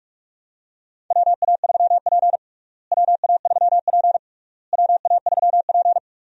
Morse code, heard as a keyed tone, about 41 wpm, 700 Hz. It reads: WA3P WA3P WA3P
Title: USA Call Signs - 4 Characters - Repeated 3x - 45wpm